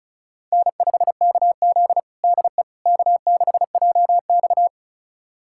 Morse code, heard as a keyed tone, about 36 wpm, 700 Hz. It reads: N5KZ DE K6JX
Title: USA Callsign DE Callsign - 4 Characters - 35wpm